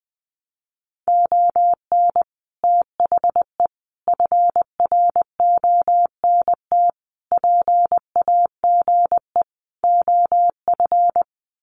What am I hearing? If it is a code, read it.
Morse code, heard as a keyed tone, about 20 wpm, 700 Hz. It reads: ON THE FRONT PAGE OF
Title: Sets of 5 Words Encore 20wpm